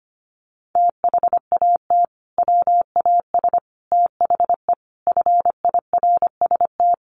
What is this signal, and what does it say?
Morse code, heard as a keyed tone, about 25 wpm, 700 Hz. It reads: THAT WAS THE FIRST
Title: Sentences from Top 300 Words 25wpm